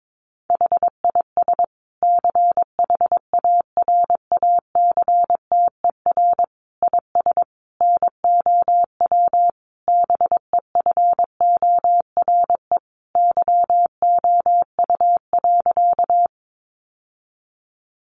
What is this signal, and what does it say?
Morse code, heard as a keyed tone, about 22 wpm, 700 Hz. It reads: HIS CHARACTER IS NOW BEFORE YOU.
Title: Sentences from Top 1200 Words 22wpm